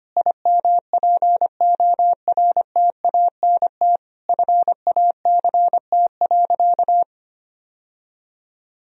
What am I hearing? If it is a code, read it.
Morse code, heard as a keyed tone, about 25 wpm, 700 Hz. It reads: IMPORTANT FACT.
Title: Sentences from Top 1000 Words - Review Part 3 - 25wpm